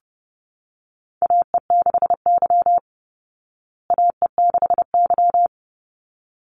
Morse code, heard as a keyed tone, about 30 wpm, 700 Hz. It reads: AE6Y AE6Y
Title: Call Signs Repeated 3x - 30wpm